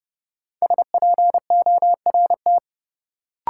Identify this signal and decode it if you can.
Morse code, heard as a keyed tone, about 32 wpm, 700 Hz. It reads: SPORT
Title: Sentences from Top 1500 Words - New Word Drill Alt - 30wpm